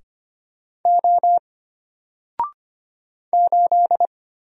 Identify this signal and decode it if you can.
Morse code, heard as a keyed tone, about 25 wpm, 700 Hz. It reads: O 8
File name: Single-Letter-Number-25wpm